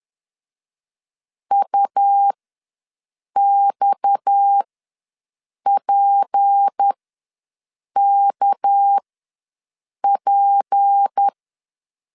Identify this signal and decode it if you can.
Morse code, heard as a keyed tone, about 10 wpm, spaced as for 6 wpm, 800 Hz. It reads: UXPKP
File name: LEZIONI / 6 PKX0 / 6 PKX ws 10wpm 3706